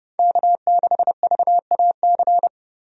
Morse code, heard as a keyed tone, about 30 wpm, 700 Hz. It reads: K6VAC
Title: USA Call Signs - 5 Characters - 30wpm